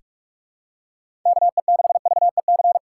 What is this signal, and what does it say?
Morse code, heard as a keyed tone, about 46 wpm, 700 Hz. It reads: KE6VEC
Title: USA Callsign DE Callsign 45wpm